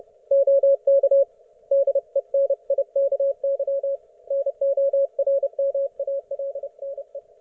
Morse code, heard as a keyed tone, about 30 wpm, 550 Hz. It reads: OK DENIKY NORMALNE
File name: Archiv / CUC181 / ok1fhi cul181